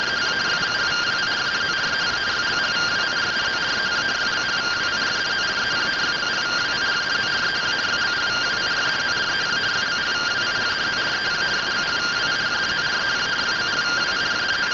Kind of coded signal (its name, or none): two-tone teleprinter signal